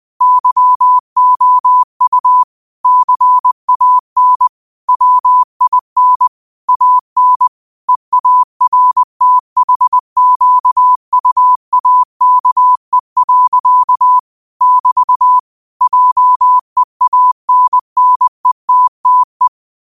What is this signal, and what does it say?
Morse code, heard as a keyed tone, about 20 wpm, 1000 Hz. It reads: YOU CAN WIN AN EARTHQUAKE. <BT> JEANNETTE